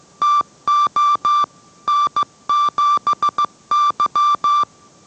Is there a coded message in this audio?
Yes: Morse code from 0.2 to 4.6 s, heard as a keyed tone, about 15 wpm, 1170 Hz. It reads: TO N7Y